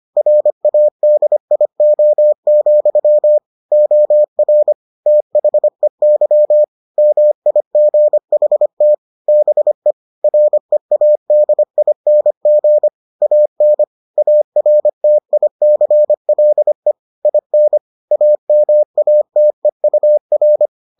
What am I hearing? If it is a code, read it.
Morse code, heard as a keyed tone, about 25 wpm, 600 Hz. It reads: RADIO, OR THEY MIGHT BE READING AN ARTICLE IN AMATEUR